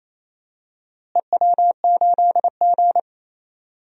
Morse code, heard as a keyed tone, about 28 wpm, 700 Hz. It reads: EW8G
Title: DXCC-100 Call Signs - 4 Characters - Repeated 3x - 28wpm